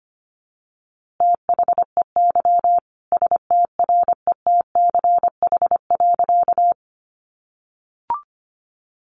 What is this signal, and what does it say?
Morse code, heard as a keyed tone, about 25 wpm, 700 Hz. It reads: THEY STRETCH.
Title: Easy Phrases 25wpm